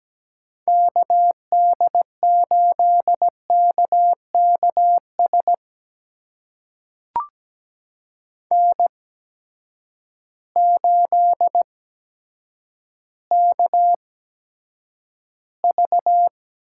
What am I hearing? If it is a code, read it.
Morse code, heard as a keyed tone, about 17 wpm, spaced as for 8 wpm, 700 Hz. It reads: KD8KKS N8KV
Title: USA Call Signs - 8x Character Spacing - 17wpm